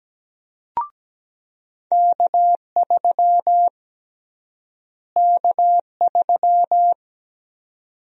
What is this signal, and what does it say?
Morse code, heard as a keyed tone, about 17 wpm, 700 Hz. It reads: K3 K3
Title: USA Call Sign Prefixes - Repeated 2x - 17wpm